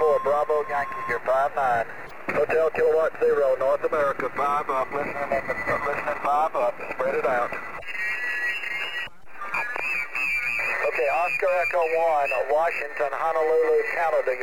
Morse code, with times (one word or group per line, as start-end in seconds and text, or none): none